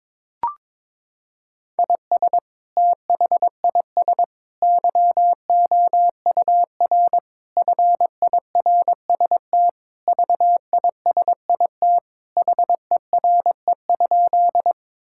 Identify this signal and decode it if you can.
Morse code, heard as a keyed tone, about 21 wpm, 700 Hz. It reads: IS THIS YOUR FIRST VISIT HERE?
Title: Sentences from Top 1000 Words - Review Part 6 - 22wpm